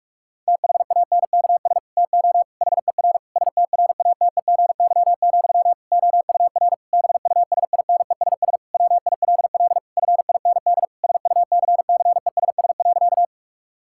Morse code, heard as a keyed tone, about 46 wpm, 700 Hz. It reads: THANKS TO HER STRATEGY, OUR BUSINESS WILL FIND SUCCESS.
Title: Sentences from Top 1000 Words - Review Part 5 - 45wpm